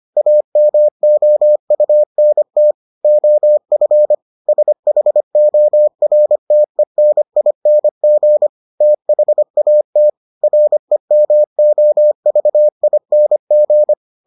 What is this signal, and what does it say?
Morse code, heard as a keyed tone, about 25 wpm, 600 Hz. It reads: AMOUNT OF SHORTENING THAT REMOVING